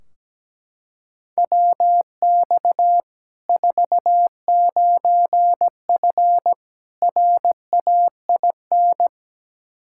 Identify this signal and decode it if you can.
Morse code, heard as a keyed tone, about 17 wpm, 700 Hz. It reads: WX 49F RAIN
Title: Practice QSO - Exchange 2 - Info - 17wpm